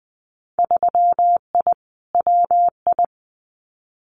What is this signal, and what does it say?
Morse code, heard as a keyed tone, about 19 wpm, 700 Hz. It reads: 3I WI
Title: Winter Field Day Exchanges 20wpm